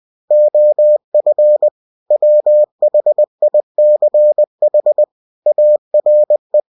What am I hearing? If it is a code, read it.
Morse code, heard as a keyed tone, about 20 wpm, 600 Hz. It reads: OF WHICH ARE
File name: cw-books / winnie-the-pooh / winnie-20wpm-page-0122